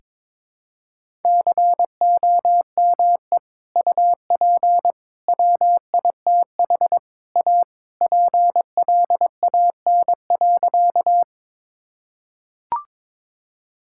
Morse code, heard as a keyed tone, about 22 wpm, 700 Hz. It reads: COME UP WITH A PLAN.
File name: practice / 22wpm / Sentences with 3-Word Phrases - 22wpm